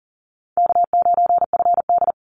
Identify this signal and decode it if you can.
Morse code, heard as a keyed tone, about 40 wpm, 700 Hz. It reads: K9FD
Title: Call Signs Repeated 3x - 40wpm